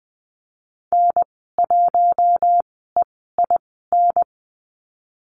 Morse code, heard as a keyed tone, about 20 wpm, spaced as for 14 wpm, 700 Hz. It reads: N1EIN
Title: 042 Lesson 8g - Number 1 - Call Signs 20wpm